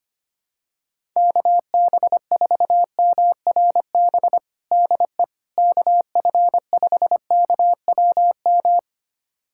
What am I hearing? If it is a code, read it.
Morse code, heard as a keyed tone, about 25 wpm, 700 Hz. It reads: KB4MRB DE KF5KWM
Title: USA Callsign DE Callsign 25wpm